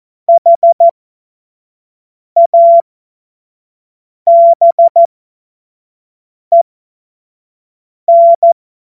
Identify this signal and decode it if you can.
Morse code, heard as a keyed tone, about 14 wpm, spaced as for 5 wpm, 680 Hz. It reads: HABEN